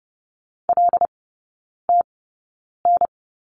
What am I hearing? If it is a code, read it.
Morse code, heard as a keyed tone, about 30 wpm, spaced as for 9 wpm, 690 Hz. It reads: LTN
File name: Lesson 16e - Letter c - Three Characters 30wpm